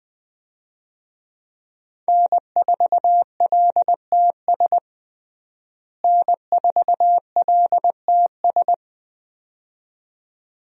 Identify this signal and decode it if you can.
Morse code, heard as a keyed tone, about 20 wpm, 700 Hz. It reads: N4LTS N4LTS
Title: USA Call Signs - 5 Characters - Repeated 2x - 20wpm